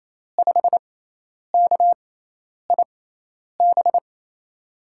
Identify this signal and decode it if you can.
Morse code, heard as a keyed tone, about 28 wpm, spaced as for 10 wpm, 700 Hz. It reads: 5KIB
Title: USA Call Signs - 6 Characters - 6x Character Spacing - 28wpm